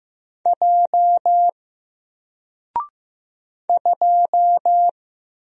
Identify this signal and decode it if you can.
Morse code, heard as a keyed tone, about 15 wpm, 700 Hz. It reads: J 2
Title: Single Letter-Number 15wpm